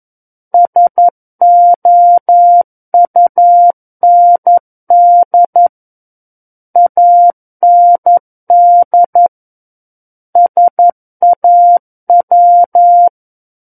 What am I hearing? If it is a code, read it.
Morse code, heard as a keyed tone, about 11 wpm, 700 Hz. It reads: SOUND AND SAW
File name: Book One / 02-Chapter 2 - 11 WPM